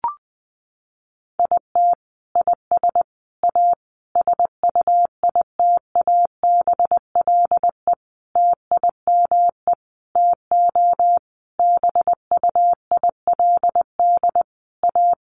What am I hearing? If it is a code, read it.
Morse code, heard as a keyed tone, about 20 wpm, 700 Hz. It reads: IT IS A SUITABLE TIME TO BUILD A
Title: Sentences from Top 1400 Words 20wpm